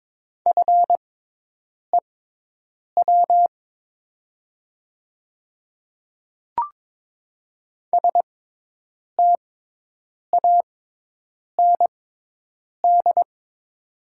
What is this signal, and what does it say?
Morse code, heard as a keyed tone, about 23 wpm, spaced as for 8 wpm, 700 Hz. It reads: FEW STAND
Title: 6x Character Spacing - Top 300 Words - 22wpm